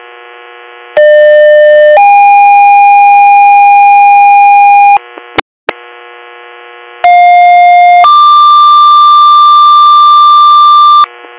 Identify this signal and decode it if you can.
Two-tone page, heard as a one-second tone, then a three-second tone, about 600 then 810 Hz, about 710 then 1180 Hz. It is heard from 1.0 to 11.0 s.